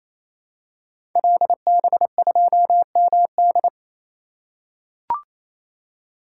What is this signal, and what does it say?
Morse code, heard as a keyed tone, about 27 wpm, 700 Hz. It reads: LB2MD
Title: Code Groups - 5 Letter:Numbers - 28wpm